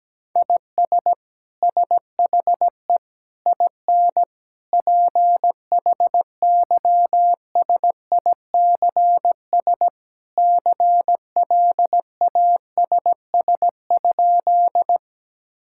Morse code, heard as a keyed tone, about 17 wpm, 700 Hz. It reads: IS SHE IN PHYSICS CLASS?
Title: Sentences from Top 1000 Words - Review Part 5 - 17wpm